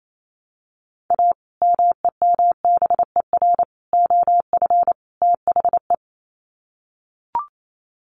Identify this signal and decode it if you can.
Morse code, heard as a keyed tone, about 28 wpm, 700 Hz. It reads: A MEMBER OF THE
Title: Sets of 4 Words 28wpm